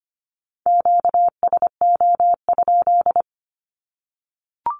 Morse code, heard as a keyed tone, about 25 wpm, 700 Hz. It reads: QSO?
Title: Q-Codes 25wpm